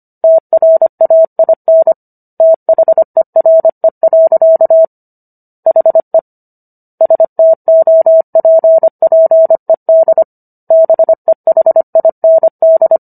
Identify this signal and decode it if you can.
Morse code, heard as a keyed tone, about 25 wpm, 650 Hz. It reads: TRAIN THERE. HE STOPPED BEHIND